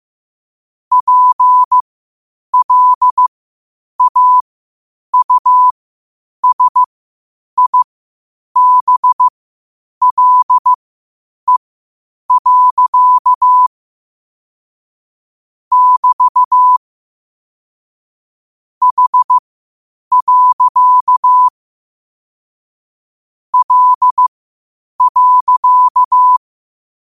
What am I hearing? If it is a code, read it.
Morse code, heard as a keyed tone, about 15 wpm, spaced as for 8 wpm, 1000 Hz. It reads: PLAUSIBLE. <BT> H. L.
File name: qotdfiles / 8wpm